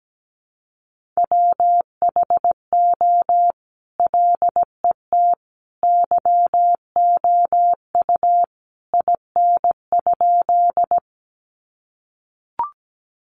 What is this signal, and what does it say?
Morse code, heard as a keyed tone, about 17 wpm, 700 Hz. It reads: WHO LET YOU IN?
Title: Sentences from Top 300 Words 17wpm